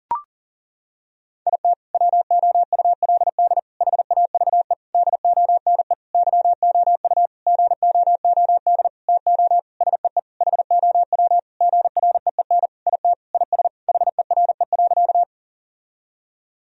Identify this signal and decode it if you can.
Morse code, heard as a keyed tone, about 40 wpm, 700 Hz. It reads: IT WOULD HAVE DONE YOU GOOD TO SEE HOW GREEN IT IS HERE.
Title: Sentences from Top 1000 Words - Review Part 2 - 40wpm